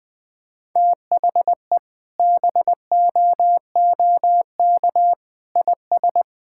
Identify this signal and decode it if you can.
Morse code, heard as a keyed tone, about 21 wpm, 700 Hz. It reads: THE BOOK IS
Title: Sentences with 3-Word Phrases - 20wpm